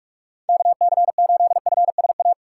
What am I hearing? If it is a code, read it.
Morse code, heard as a keyed tone, about 41 wpm, 700 Hz. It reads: KC8FSA
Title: USA Call Signs - 6 Characters - 2x Character Spacing - 45wpm